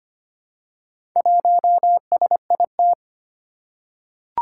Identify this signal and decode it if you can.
Morse code, heard as a keyed tone, about 25 wpm, 700 Hz. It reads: 1SIT
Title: Four Challenging Letter-Number Combinations 25wpm